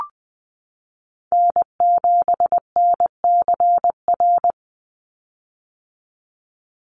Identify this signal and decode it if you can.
Morse code, heard as a keyed tone, about 20 wpm, 700 Hz. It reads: N7NCR
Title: USA Call Signs 20wpm